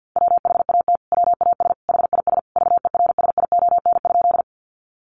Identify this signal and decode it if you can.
Morse code, heard as a keyed tone, about 50 wpm, 700 Hz. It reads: WHAT WAS HIS VERSION?
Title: Sentences from Top 1400 Words 50wpm